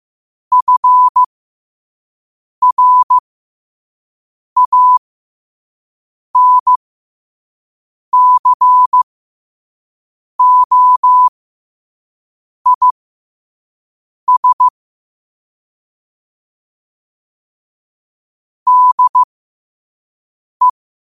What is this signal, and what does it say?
Morse code, heard as a keyed tone, about 15 wpm, spaced as for 5 wpm, 1000 Hz. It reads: FRANCOIS DE
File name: qotdfiles / 5wpm